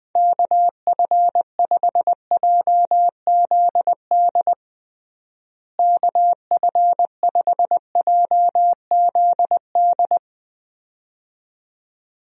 Morse code, heard as a keyed tone, about 20 wpm, 700 Hz. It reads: KF5JZD KF5JZD
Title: USA Call Signs - 6 Characters - Repeated 2x - 20wpm